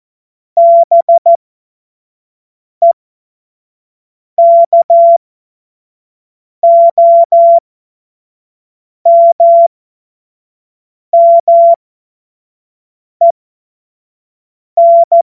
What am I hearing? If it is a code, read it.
Morse code, heard as a keyed tone, about 14 wpm, spaced as for 5 wpm, 680 Hz. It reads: BEKOMMEN
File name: Morsekurs / Witze / Witz 16. Dezember 2025 25 F70